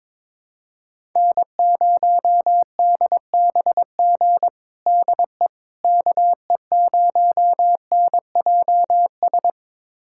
Morse code, heard as a keyed tone, about 22 wpm, 700 Hz. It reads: N0DBG DE KE0NJS
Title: USA Callsign DE Callsign 22wpm